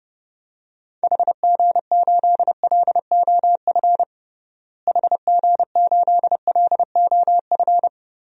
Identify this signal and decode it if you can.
Morse code, heard as a keyed tone, about 30 wpm, 700 Hz. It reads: HG8LOF HG8LOF
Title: DXCC-100 Call Signs - 6 Characters - Repeated 2x - 30wpm